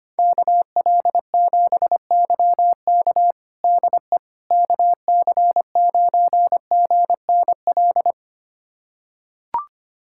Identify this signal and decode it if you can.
Morse code, heard as a keyed tone, about 25 wpm, 700 Hz. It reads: KL7YK DE KC9GNL
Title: USA Callsign DE Callsign 25wpm